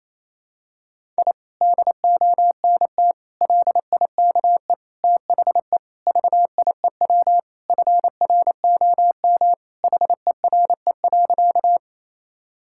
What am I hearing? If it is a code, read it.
Morse code, heard as a keyed tone, about 27 wpm, 700 Hz. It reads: I DONT LIKE THE VIEW FROM HERE.
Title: Sentences from Top 1000 Words - Review Part 6 - 28wpm